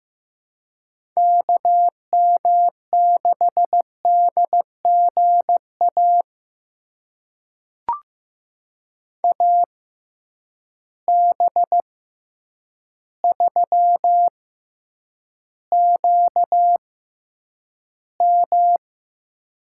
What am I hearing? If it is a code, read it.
Morse code, heard as a keyed tone, about 15 wpm, spaced as for 8 wpm, 700 Hz. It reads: KM6DGA AB3QM
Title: USA Call Signs - 6x Character Spacing - 15wpm